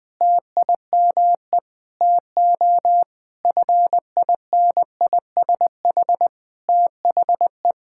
Morse code, heard as a keyed tone, about 21 wpm, 700 Hz. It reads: TIME TO FINISH THE
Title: Sentences with 7-Word Phrases - 20wpm